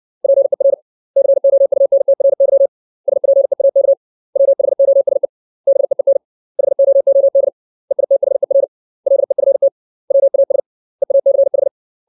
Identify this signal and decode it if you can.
Morse code, heard as a keyed tone, about 60 wpm, 550 Hz. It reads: PER COUNTRY SPEAK WHOSE BEEN HOOD EITHER LEFT WAS EACH
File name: 632 words / 632 Words 60WPM